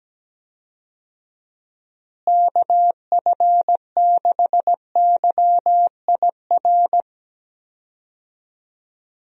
Morse code, heard as a keyed tone, about 17 wpm, 700 Hz. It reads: KF6YIR
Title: USA Call Signs - 6 Characters - Repeated 2x - 17wpm